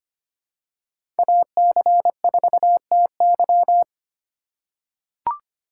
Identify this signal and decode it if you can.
Morse code, heard as a keyed tone, about 25 wpm, 700 Hz. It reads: AC4TY
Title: USA Call Signs - 5 Characters - 2x Character Spacing - 25wpm